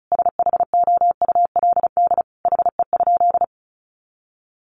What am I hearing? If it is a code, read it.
Morse code, heard as a keyed tone, about 34 wpm, 700 Hz. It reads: SHOULD HE?
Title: Sentences from Top 300 Words 35wpm